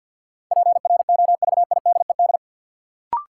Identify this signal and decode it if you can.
Morse code, heard as a keyed tone, about 50 wpm, 700 Hz. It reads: PROVIDED
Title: Sentences from Top 1500 Words - New Word Drill - 50wpm